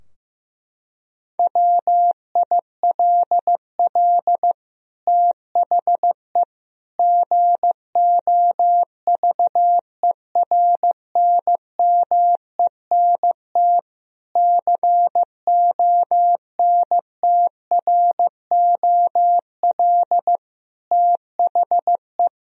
Morse code, heard as a keyed tone, about 15 wpm, 700 Hz. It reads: WILL THE GOVERNMENT CONTROL THE
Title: Sentences from Top 1000 Words - Review Part 3 - 15wpm